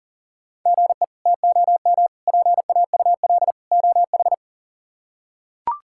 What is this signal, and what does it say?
Morse code, heard as a keyed tone, about 40 wpm, 700 Hz. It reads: GE TOM PAUL OH
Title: K1USN SST QSO Practice - 40wpm